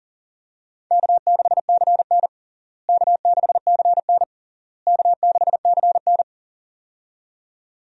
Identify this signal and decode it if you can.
Morse code, heard as a keyed tone, about 40 wpm, 700 Hz. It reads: K6CN K6CN K6CN
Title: USA Call Signs - 4 Characters - Repeated 3x - 40wpm